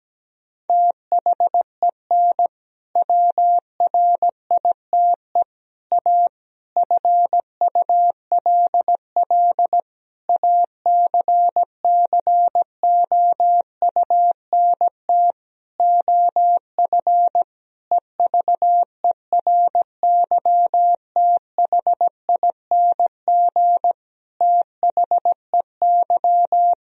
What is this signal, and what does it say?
Morse code, heard as a keyed tone, about 17 wpm, 700 Hz. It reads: THEN WRITE A FULL ACCOUNT OF EVERYTHING THEY
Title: Sentences from Top 1000 Words - Review Part 3 - 17wpm